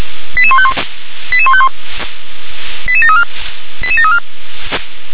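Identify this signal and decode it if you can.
Five-tone selective call (ZVEI), heard as quick runs of five tones, tones of about 80 ms. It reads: 88151 88151 88734 88734
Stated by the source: ZVEI five-tone selective call received on SDR.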